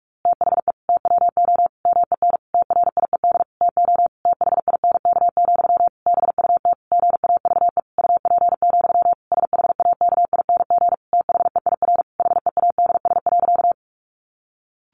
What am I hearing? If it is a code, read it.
Morse code, heard as a keyed tone, about 44 wpm, 700 Hz. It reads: THE TWO MEN TRIED TO THINK, BUT GAVE UP, SHAKING THEIR HEADS.
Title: A Sound Of Thunder 45wpm